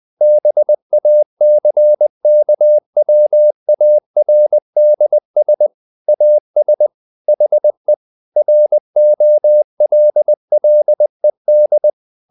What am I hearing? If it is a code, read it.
Morse code, heard as a keyed tone, about 20 wpm, 600 Hz. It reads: BACKWARDS AS HE ROLLED
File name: the old man and the sea part 2 20wpm